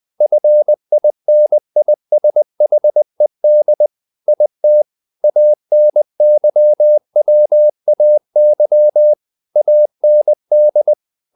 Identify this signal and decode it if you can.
Morse code, heard as a keyed tone, about 20 wpm, 600 Hz. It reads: FINISHED IT ANYWAY AND
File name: the old man and the sea part 2 20wpm